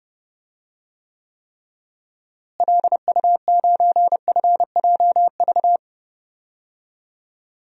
Morse code, heard as a keyed tone, about 30 wpm, 700 Hz. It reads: LU9FJV
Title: DXCC-100 Call Signs - 6 Characters - Repeated 3x - 30wpm